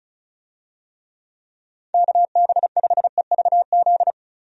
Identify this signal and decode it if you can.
Morse code, heard as a keyed tone, about 36 wpm, 700 Hz. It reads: KB5EVZ
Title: USA Call Signs - 6 Characters - Repeated 2x - 35wpm